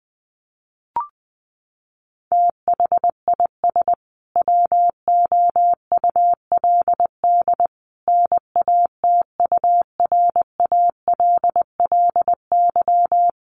Morse code, heard as a keyed tone, about 20 wpm, 700 Hz. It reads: THIS WOULD NATURALLY
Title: Sentences from Top 1300 Words 20wpm